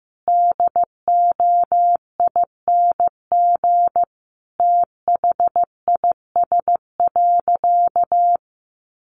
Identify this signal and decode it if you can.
Morse code, heard as a keyed tone, about 14 wpm, 700 Hz. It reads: DOING THIS.
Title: Easy Phrases 15wpm